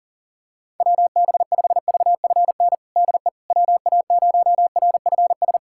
Morse code, heard as a keyed tone, about 37 wpm, 700 Hz. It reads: WB5VFN DE WA0RFS
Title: USA Callsign DE Callsign - 6 Characters - 40wpm